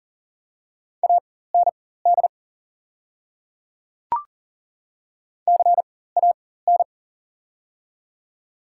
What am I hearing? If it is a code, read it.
Morse code, heard as a keyed tone, about 40 wpm, spaced as for 19 wpm, 700 Hz. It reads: AND CAN
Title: 4x Character Spacing - Top 300 Words - 40wpm